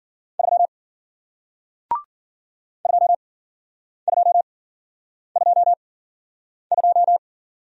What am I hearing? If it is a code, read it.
Morse code, heard as a keyed tone, about 53 wpm, spaced as for 9 wpm, 700 Hz. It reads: 3 2222